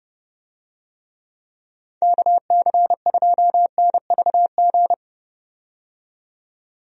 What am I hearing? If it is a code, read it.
Morse code, heard as a keyed tone, about 30 wpm, 700 Hz. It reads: KC2NVG
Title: USA Call Signs - 6 Characters - 30wpm